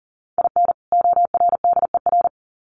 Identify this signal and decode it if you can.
Morse code, heard as a keyed tone, about 40 wpm, 700 Hz. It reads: IN ORDER
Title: Sets of 2 Words 40wpm